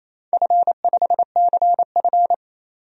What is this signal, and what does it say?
Morse code, heard as a keyed tone, about 27 wpm, 700 Hz. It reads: F5CF
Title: DXCC-100 Call Signs - 4 Characters - 28wpm